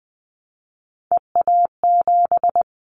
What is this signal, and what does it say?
Morse code, heard as a keyed tone, about 20 wpm, 700 Hz. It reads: EA7
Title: Callsign Prefixes for Top 25 DXCC - 20wpm